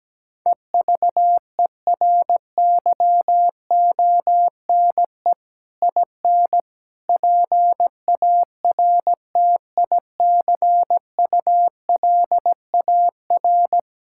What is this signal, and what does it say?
Morse code, heard as a keyed tone, about 17 wpm, 700 Hz. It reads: EVERYONE IN PARTICULAR
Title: Sentences from Top 1600 Words 17wpm